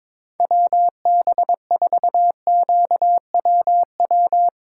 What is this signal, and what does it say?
Morse code, heard as a keyed tone, about 22 wpm, 700 Hz. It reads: WB4QWW
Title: Speed Racing - USA 2x3 Call Signs 22wpm